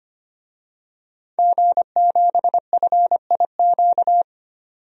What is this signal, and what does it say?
Morse code, heard as a keyed tone, about 25 wpm, 700 Hz. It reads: G7FIQ
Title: DXCC-100 Call Signs - 5 Characters - 25wpm